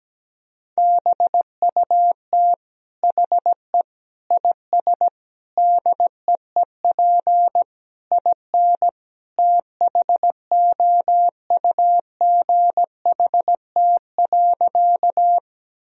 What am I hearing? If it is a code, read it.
Morse code, heard as a keyed tone, about 17 wpm, 700 Hz. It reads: BUT HE IS DEEP IN THOUGHT.
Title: Sentences from Top 1000 Words - Review Part 3 - 17wpm